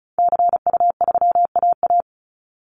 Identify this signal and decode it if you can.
Morse code, heard as a keyed tone, about 34 wpm, 700 Hz. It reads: CU3AA
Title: Call Signs Repeated 3x - 35wpm